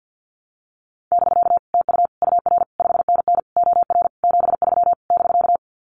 Morse code, heard as a keyed tone, about 50 wpm, 700 Hz. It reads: <BK> TU UR 5NN OR 73 <BK>
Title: Common QSO Phrases Encore 50wpm